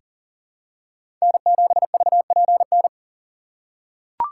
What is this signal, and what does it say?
Morse code, heard as a keyed tone, about 40 wpm, 700 Hz. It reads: N7VPN
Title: USA Call Signs - 5 Characters - 3x Character Spacing - 40wpm